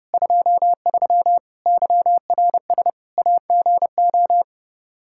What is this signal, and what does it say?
Morse code, heard as a keyed tone, about 32 wpm, 700 Hz. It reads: 23 YRS AGO
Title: Practice QSO - Exchange 3 - More Info - 30wpm